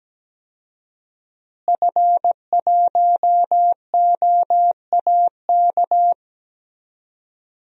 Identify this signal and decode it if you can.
Morse code, heard as a keyed tone, about 17 wpm, 700 Hz. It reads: F1OAK
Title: DXCC-100 Call Signs - 5 Characters - Repeated 3x - 17wpm